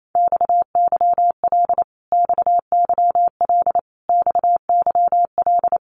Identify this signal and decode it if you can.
Morse code, heard as a keyed tone, about 28 wpm, 700 Hz. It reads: XYL XYL XYL
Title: Abbreviations Repeated 3x - 28wpm